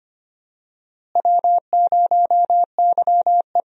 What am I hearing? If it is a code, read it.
Morse code, heard as a keyed tone, about 25 wpm, 700 Hz. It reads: W0YE
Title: USA Call Signs - 4 Characters - Repeated 3x - 25wpm